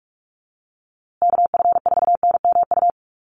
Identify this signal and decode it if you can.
Morse code, heard as a keyed tone, about 44 wpm, 700 Hz. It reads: KF4NMU
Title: USA Call Signs Repeated 3x 45wpm